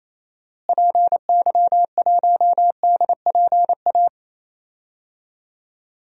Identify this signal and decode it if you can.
Morse code, heard as a keyed tone, about 28 wpm, 700 Hz. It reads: PY1DPA
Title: DXCC-100 Call Signs - 6 Characters - Repeated 2x - 28wpm